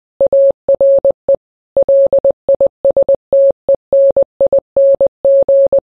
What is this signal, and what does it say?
Morse code, heard as a keyed tone, about 20 wpm, 550 Hz. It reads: ARE LISTENING
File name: radio / Imaginary Stations / KSPY 9395 kHz 29.1.23